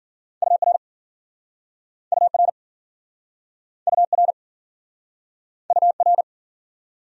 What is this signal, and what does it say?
Morse code, heard as a keyed tone, about 53 wpm, 700 Hz. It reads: UR UR UR UR